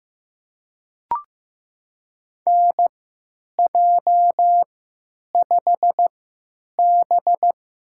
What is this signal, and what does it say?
Morse code, heard as a keyed tone, about 15 wpm, spaced as for 9 wpm, 700 Hz. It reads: NJ5B
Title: USA Call Signs - 4 Characters - 3x Character Spacing - 15wpm